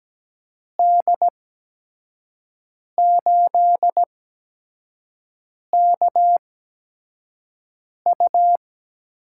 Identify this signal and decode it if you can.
Morse code, heard as a keyed tone, about 17 wpm, spaced as for 5 wpm, 700 Hz. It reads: D8KU
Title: USA Call Signs - 6 Characters - 8x Character Spacing - 17wpm